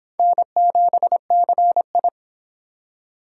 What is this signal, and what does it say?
Morse code, heard as a keyed tone, about 31 wpm, 700 Hz. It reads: N7CI